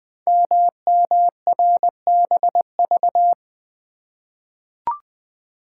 Morse code, heard as a keyed tone, about 20 wpm, 700 Hz. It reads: MMRBV
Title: Code Groups - 5 Letters - 20wpm